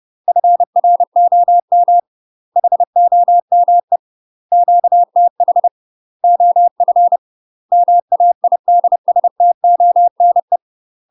Morse code, heard as a keyed tone, about 30 wpm, 700 Hz. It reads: FROM HOME QTH OF MAIDSTONE
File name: UK MASTER QSOs TRAINING / 30WPM / UK MASTER QSOs 0043